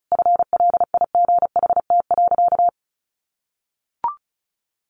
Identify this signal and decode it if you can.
Morse code, heard as a keyed tone, about 34 wpm, 700 Hz. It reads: FLIGHT.
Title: Sentences from Top 1400 Words 35wpm